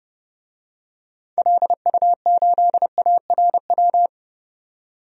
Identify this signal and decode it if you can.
Morse code, heard as a keyed tone, about 30 wpm, 700 Hz. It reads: LU8ARW
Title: DXCC-100 Call Signs - 6 Characters - 30wpm